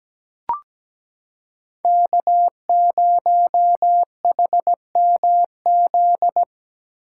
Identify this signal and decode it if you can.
Morse code, heard as a keyed tone, about 17 wpm, 700 Hz. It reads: K0HMZ
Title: K1USN SST QSO Practice - 17wpm